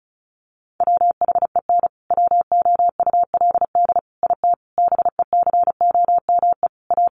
Morse code, heard as a keyed tone, about 34 wpm, 700 Hz. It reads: WHEN WOULD IT BECOME A
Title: Sentences from Top 1200 Words 35wpm